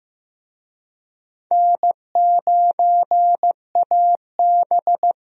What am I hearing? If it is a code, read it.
Morse code, heard as a keyed tone, about 15 wpm, 700 Hz. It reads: N9AB
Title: USA Call Signs - 4 Characters - Repeated 2x - 15wpm